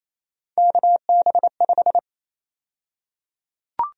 Morse code, heard as a keyed tone, about 28 wpm, 700 Hz. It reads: KB5
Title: USA Call Sign Prefixes - 28wpm